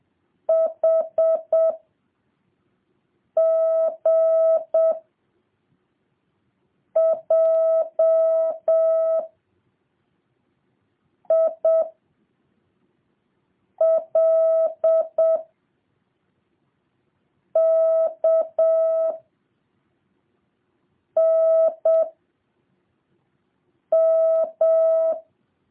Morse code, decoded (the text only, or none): H G J I L K N M